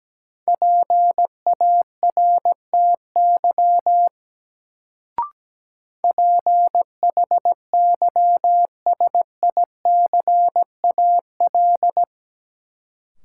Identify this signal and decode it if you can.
Morse code, heard as a keyed tone, about 17 wpm, 700 Hz. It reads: PARTY PHYSICAL
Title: Sentences from Top 1400 Words - New Word Drill Alt - 17wpm